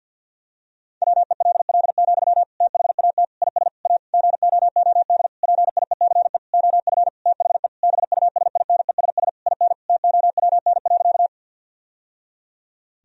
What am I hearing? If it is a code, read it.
Morse code, heard as a keyed tone, about 50 wpm, 700 Hz. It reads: WELL, THAT IS A GOOD PIECE OF THE BUSINESS IN TOWN.
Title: Sentences from Top 1000 Words - Review Part 2 - 50wpm